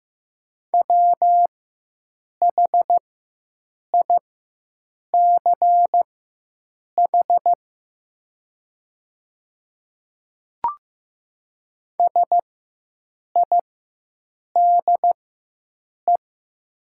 Morse code, heard as a keyed tone, about 15 wpm, spaced as for 7 wpm, 700 Hz. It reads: WHICH SIDE
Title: 4x Character Spacing - Top 100 Words - 15wpm